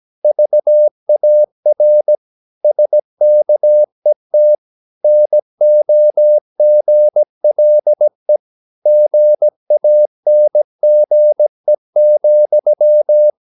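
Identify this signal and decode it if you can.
Morse code, heard as a keyed tone, about 17 wpm, 600 Hz. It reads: VAR SKET NOGLE GANGE,
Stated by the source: Slow (QRS) Morse practice at 17 wpm, from CW med Gnister.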